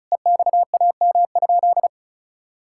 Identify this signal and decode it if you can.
Morse code, heard as a keyed tone, about 36 wpm, 700 Hz. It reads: EXAM?
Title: Sentences from Top 1000 Words - Review Part 5 - 35wpm